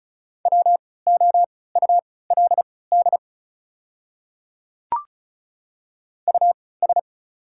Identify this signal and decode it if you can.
Morse code, heard as a keyed tone, about 36 wpm, spaced as for 20 wpm, 700 Hz. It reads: WOULD US
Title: 3x Character Spacing - Top 300 Words - 35wpm